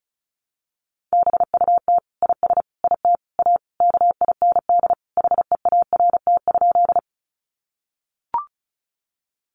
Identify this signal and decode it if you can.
Morse code, heard as a keyed tone, about 34 wpm, 700 Hz. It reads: BUT IS IT A KIND HEART?
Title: Sentences from Top 1200 Words 35wpm